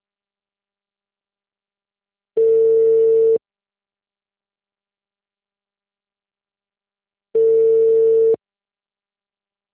Ringback tone (China, 450 Hz).